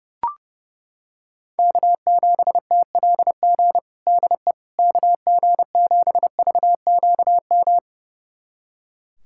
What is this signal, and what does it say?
Morse code, heard as a keyed tone, about 32 wpm, 700 Hz. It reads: K7TLG DE KG7VQM
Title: USA Callsign DE Callsign 30wpm